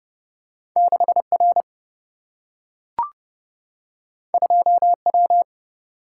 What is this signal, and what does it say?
Morse code, heard as a keyed tone, about 30 wpm, 700 Hz. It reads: 6R 2W
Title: Single-Letter Single-Number 30wpm